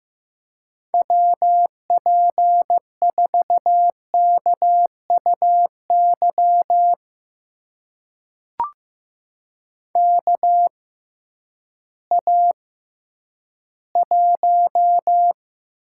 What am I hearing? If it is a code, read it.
Morse code, heard as a keyed tone, about 15 wpm, spaced as for 10 wpm, 700 Hz. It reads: WP4KUY KA1
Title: USA Call Signs - 6 Characters - 6x Character Spacing - 15wpm